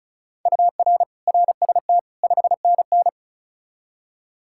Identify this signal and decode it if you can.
Morse code, heard as a keyed tone, about 46 wpm, 700 Hz. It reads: UR RST 5NN